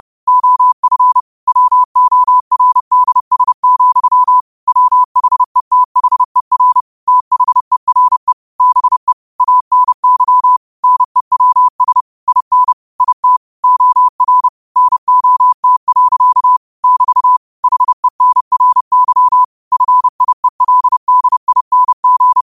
Morse code, heard as a keyed tone, about 30 wpm, 1000 Hz. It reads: OF WORDS, WHETHER THERE BE ANY NEWS IN IT OR NOT. <BT> HENRY FIELDING
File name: qotdfiles / 30wpm